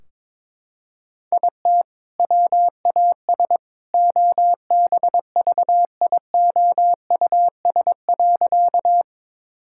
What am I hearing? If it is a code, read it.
Morse code, heard as a keyed tone, about 22 wpm, 700 Hz. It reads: IT WAS OBVIOUS.
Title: Sentences from Top 1600 Words 22wpm